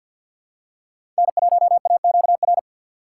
Morse code, heard as a keyed tone, about 46 wpm, 700 Hz. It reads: N1AQR
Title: USA Call Signs - 5 Characters - 2x Character Spacing - 50wpm